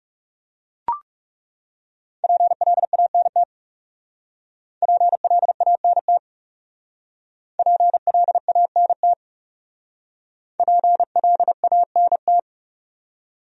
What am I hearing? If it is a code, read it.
Morse code, heard as a keyed tone, about 40 wpm, 700 Hz. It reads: PLANT PLANT PLANT PLANT